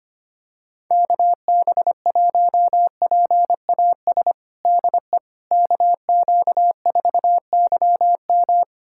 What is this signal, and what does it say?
Morse code, heard as a keyed tone, about 25 wpm, 700 Hz. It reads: KB1PAS DE KQ4YM
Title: USA Callsign DE Callsign 25wpm